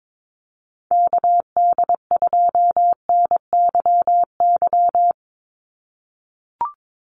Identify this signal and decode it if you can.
Morse code, heard as a keyed tone, about 22 wpm, 700 Hz. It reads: KD2NYY
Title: USA Call Signs 22wpm